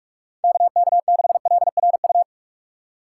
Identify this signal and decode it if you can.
Morse code, heard as a keyed tone, about 46 wpm, 700 Hz. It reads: KK6LRU
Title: USA Call Signs - 6 Characters - 6x Character Spacing - 45wpm